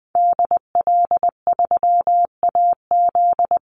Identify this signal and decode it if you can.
Morse code, heard as a keyed tone, about 20 wpm, 700 Hz. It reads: DL3AZ
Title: Call Signs Repeated 3x - 20wpm